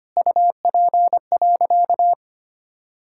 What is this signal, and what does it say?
Morse code, heard as a keyed tone, about 25 wpm, 700 Hz. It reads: UP.
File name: practice / 25wpm / Sentences with 4-Word Phrases - 25wpm